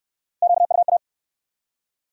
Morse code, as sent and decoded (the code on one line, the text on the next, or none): --.- ..- .-
QUA